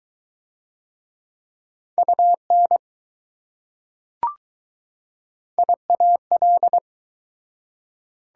Morse code, heard as a keyed tone, about 20 wpm, 700 Hz. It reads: UN IAL